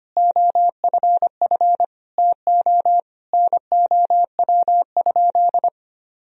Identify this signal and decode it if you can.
Morse code, heard as a keyed tone, about 25 wpm, 700 Hz. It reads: OFF TO NOW?
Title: Sentences from Top 200 Words and Abbreviations 25wpm